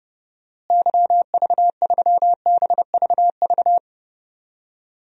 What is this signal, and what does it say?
Morse code, heard as a keyed tone, about 31 wpm, 700 Hz. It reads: YV3BVV
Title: DXCC-100 Call Signs - 6 Characters - Repeated 3x - 30wpm